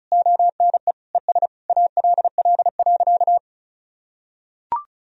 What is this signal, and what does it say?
Morse code, heard as a keyed tone, about 36 wpm, 700 Hz. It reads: ONE ES ALL.
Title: Sentences from Top 200 Words and Abbreviations 35wpm